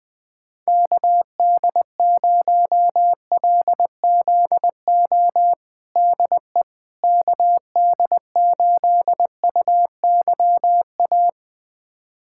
Morse code, heard as a keyed tone, about 20 wpm, 700 Hz. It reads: KD0LZO DE KD8UYA
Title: USA Callsign DE Callsign - 6 Characters - 20wpm